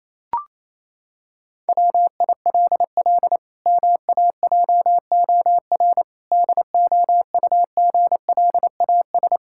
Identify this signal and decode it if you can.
Morse code, heard as a keyed tone, about 28 wpm, 700 Hz. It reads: WILL MAJOR DOUGLAS
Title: Pangram Challenge v2 - 28wpm